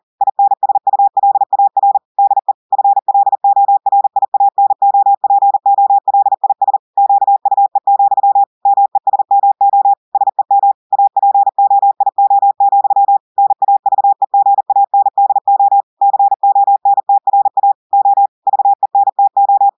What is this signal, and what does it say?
Morse code, heard as a keyed tone, about 40 wpm, 800 Hz. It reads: INSULAR DE FLORIANOPOLIS QUE, MESMO SEM APOIO, NAVEGANDO CONTRA O VENTO
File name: audios / portugais / portugais texte en clair 1 vitesse 40 mots minutes